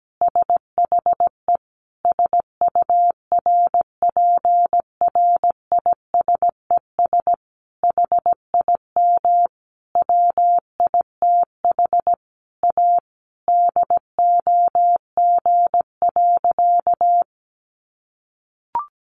Morse code, heard as a keyed tone, about 17 wpm, 700 Hz. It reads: SHE SURPRISES HIM WITH A DOG.
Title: Additional Easy Phrases 17wpm